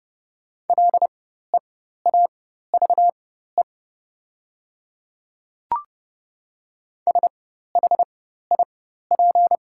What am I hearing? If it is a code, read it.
Morse code, heard as a keyed tone, about 30 wpm, spaced as for 14 wpm, 690 Hz. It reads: LEAVE SHIP
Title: 4x Character Spacing - Top 500 Words - 30wpm